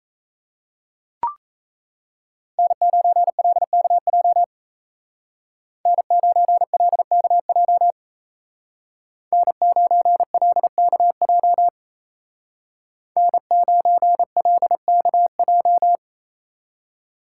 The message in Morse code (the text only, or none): N9LKJ N9LKJ N9LKJ N9LKJ